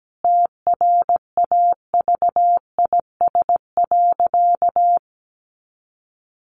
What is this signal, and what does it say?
Morse code, heard as a keyed tone, about 17 wpm, 700 Hz. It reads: TRAVIS.
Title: A Sound Of Thunder 17wpm